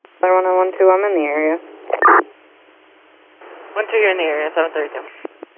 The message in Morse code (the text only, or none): none